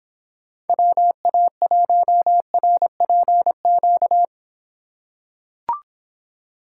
Morse code, heard as a keyed tone, about 26 wpm, 700 Hz. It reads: WA1RPQ